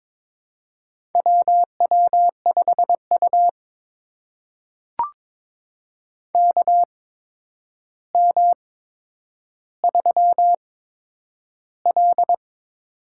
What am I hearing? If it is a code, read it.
Morse code, heard as a keyed tone, about 23 wpm, spaced as for 10 wpm, 700 Hz. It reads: WW5U KM3L
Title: USA Call Signs - 4 Characters - 8x Character Spacing - 22wpm